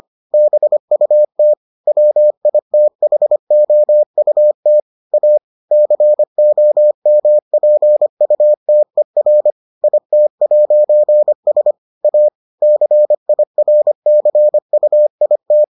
Morse code, heard as a keyed tone, about 25 wpm, 600 Hz. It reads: BUT WITHOUT A COMPUTER IT'S A CIRCUIT